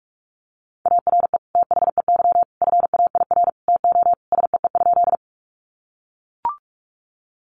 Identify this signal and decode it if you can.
Morse code, heard as a keyed tone, about 44 wpm, 700 Hz. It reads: ARE THEY FAIR TO SEE?
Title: Sentences from Top 1300 Words 45wpm